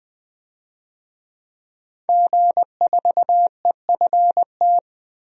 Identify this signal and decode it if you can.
Morse code, heard as a keyed tone, about 21 wpm, 700 Hz. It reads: G4EFT
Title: DXCC-100 Call Signs - 5 Characters - 20wpm